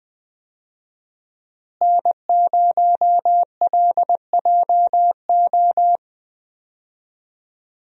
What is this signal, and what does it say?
Morse code, heard as a keyed tone, about 20 wpm, 700 Hz. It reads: N0LJO
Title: USA Call Signs - 5 Characters - Repeated 3x - 20wpm